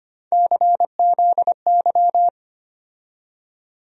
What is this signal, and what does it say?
Morse code, heard as a keyed tone, about 25 wpm, 700 Hz. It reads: CZY
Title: USA Call Sign Postfixes - 3 Letters 25wpm